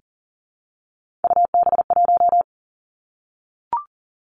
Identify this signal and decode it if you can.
Morse code, heard as a keyed tone, about 38 wpm, 690 Hz. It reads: UB1
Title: Three Challenging Letter-Number Combinations 40wpm